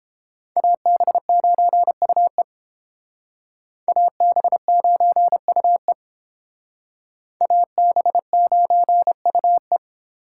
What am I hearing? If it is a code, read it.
Morse code, heard as a keyed tone, about 30 wpm, 700 Hz. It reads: AB9UE AB9UE AB9UE